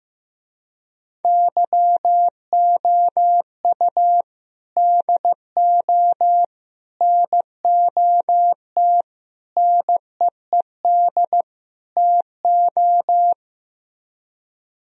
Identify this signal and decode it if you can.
Morse code, heard as a keyed tone, about 16 wpm, 700 Hz. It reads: YOU DO NOT NEED TO
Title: Speed-Racing - Sets of 5 Words 15wpm